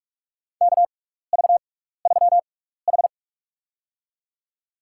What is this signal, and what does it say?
Morse code, heard as a keyed tone, about 44 wpm, spaced as for 15 wpm, 700 Hz. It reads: KV3H
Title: USA Call Signs - 4 Characters - 6x Character Spacing - 45wpm